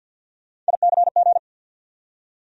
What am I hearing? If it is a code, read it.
Morse code, heard as a keyed tone, about 50 wpm, 700 Hz. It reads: ICG
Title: USA Call Sign Postfixes - 3 Letters 50wpm